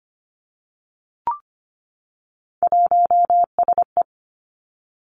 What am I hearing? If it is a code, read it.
Morse code, heard as a keyed tone, about 25 wpm, 700 Hz. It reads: 1SE
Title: Three Challenging Letter-Number Combinations 25wpm